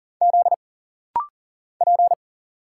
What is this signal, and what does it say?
Morse code, heard as a keyed tone, about 40 wpm, 700 Hz. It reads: Z P
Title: Single Letter-Number - Rapid-Fire 40wpm